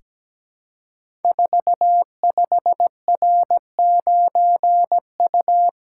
Morse code, heard as a keyed tone, about 17 wpm, 700 Hz. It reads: 45R9U
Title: Code Groups - 5 Letter:Numbers - 17wpm